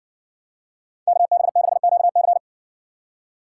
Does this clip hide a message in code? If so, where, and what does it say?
Yes: Morse code from 1.1 to 2.4 s, heard as a keyed tone, about 57 wpm, 700 Hz. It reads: KB6QC